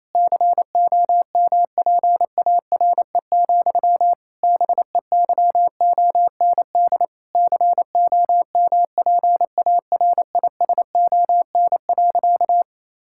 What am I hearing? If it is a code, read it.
Morse code, heard as a keyed tone, about 28 wpm, 700 Hz. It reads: COMPARE, BEYOND COMPARISON.
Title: Sentences from Top 1600 Words 28wpm